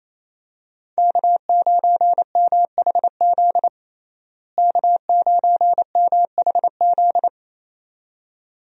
Morse code, heard as a keyed tone, about 28 wpm, 700 Hz. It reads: K9MHZ K9MHZ
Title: USA Call Signs - 5 Characters - Repeated 2x - 28wpm